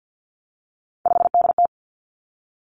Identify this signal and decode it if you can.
Morse code, heard as a keyed tone, about 50 wpm, 700 Hz. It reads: 5DT
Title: Three Challenging Letter-Number Combinations 50wpm